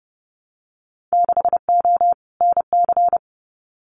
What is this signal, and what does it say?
Morse code, heard as a keyed tone, about 30 wpm, 700 Hz. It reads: 6O NC
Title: Winter Field Day Exchanges 30wpm